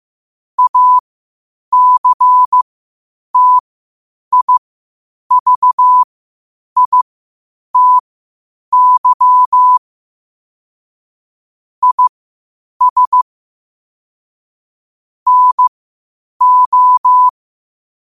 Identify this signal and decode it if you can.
Morse code, heard as a keyed tone, about 15 wpm, spaced as for 8 wpm, 1000 Hz. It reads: ACTIVITY IS NO
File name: qotdfiles / 8wpm